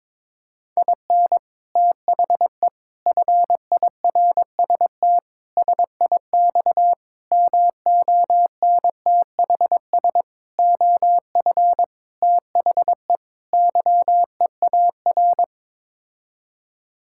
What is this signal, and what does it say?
Morse code, heard as a keyed tone, about 22 wpm, 700 Hz. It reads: IN THE FIRST SIX MONTHS OF THE YEAR
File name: practice / 22wpm / Sets of 8 Words - 22wpm